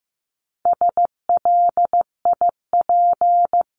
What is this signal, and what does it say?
Morse code, heard as a keyed tone, about 15 wpm, 700 Hz. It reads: SLIP
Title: Top 1751-2000 Words 15wpm